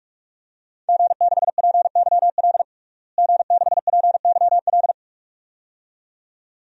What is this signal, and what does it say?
Morse code, heard as a keyed tone, about 46 wpm, 700 Hz. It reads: G6PYL G6PYL
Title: DXCC-100 Call Signs - 5 Characters - Repeated 2x - 45wpm